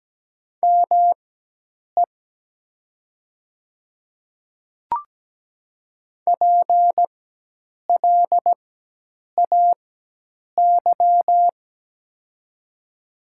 Morse code, heard as a keyed tone, about 17 wpm, spaced as for 8 wpm, 700 Hz. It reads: ME PLAY
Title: 4x Character Spacing - Top 200 Words - 17wpm